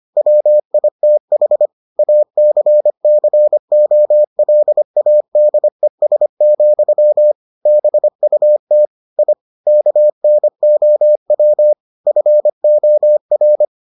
Morse code, heard as a keyed tone, about 25 wpm, 600 Hz. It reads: WITH ACCOLADES, BUT I KNOW FOR